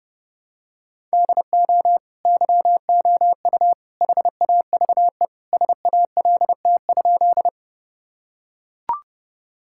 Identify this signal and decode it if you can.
Morse code, heard as a keyed tone, about 32 wpm, 700 Hz. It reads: DO YOU HAVE SALT?
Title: Sentences from Top 1000 Words - Review Part 5 - 30wpm